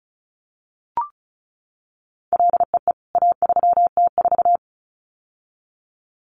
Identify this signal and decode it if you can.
Morse code, heard as a keyed tone, about 34 wpm, 700 Hz. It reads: LEE A3T4
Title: CWT Exchange 35wpm